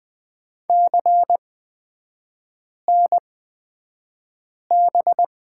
Morse code, heard as a keyed tone, about 20 wpm, spaced as for 5 wpm, 700 Hz. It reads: CNB
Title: 224 Lesson 36e - BT - Three Characters 20wpm